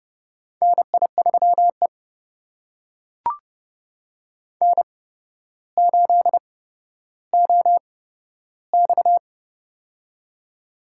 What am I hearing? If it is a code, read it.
Morse code, heard as a keyed tone, about 31 wpm, spaced as for 13 wpm, 700 Hz. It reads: NI3E N8OX
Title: USA Call Signs - 4 Characters - 8x Character Spacing - 30wpm